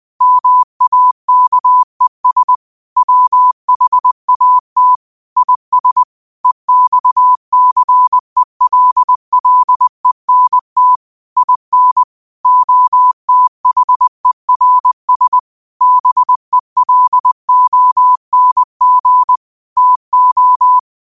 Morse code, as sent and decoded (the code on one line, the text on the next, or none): -- .- -.- . ... / .-- .... .- - / .. ... / . -..- -.-. .-.. .-.. -. - / .. -. / --- - .... .-. ... / -... .-.. --- -. --. / - ---
MAKES WHAT IS EXCELLENT IN OTHERS BELONG TO